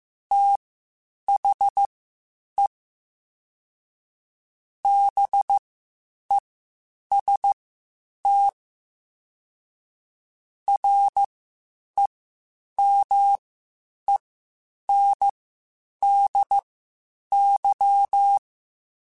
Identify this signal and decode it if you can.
Morse code, heard as a keyed tone, about 15 wpm, spaced as for 8 wpm, 800 Hz. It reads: THE BEST REMENDY
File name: Quotes Spaced / Quotes Spaced-mp3s / Temper sp @13